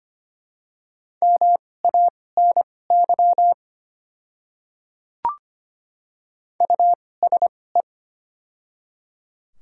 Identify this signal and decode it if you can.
Morse code, heard as a keyed tone, about 25 wpm, spaced as for 18 wpm, 690 Hz. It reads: MANY USE
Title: 2x Character Spacing - Top 100 Words - 25wpm